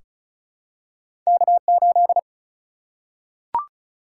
Morse code, heard as a keyed tone, about 36 wpm, 700 Hz. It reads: K8
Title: Single-Letter Single-Number 35wpm